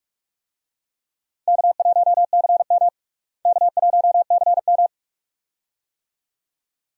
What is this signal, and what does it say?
Morse code, heard as a keyed tone, about 46 wpm, 700 Hz. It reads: K1CM K1CM
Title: USA Call Signs - 4 Characters - Repeated 2x - 45wpm